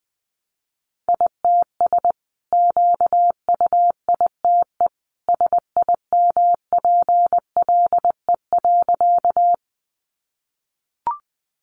Morse code, heard as a keyed tone, about 20 wpm, 700 Hz. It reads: ITS QUITE SIMPLE.
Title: Sentences from Top 700 Words 20wpm